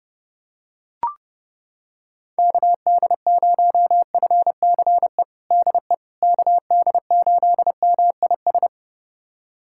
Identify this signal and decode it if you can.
Morse code, heard as a keyed tone, about 32 wpm, 700 Hz. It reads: KD0FCE DE KD8MIS
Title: USA Callsign DE Callsign 30wpm